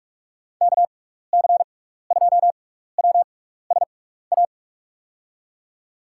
Morse code, heard as a keyed tone, about 46 wpm, spaced as for 15 wpm, 700 Hz. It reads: KC2WSA
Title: USA Call Signs - 6 Characters - 6x Character Spacing - 45wpm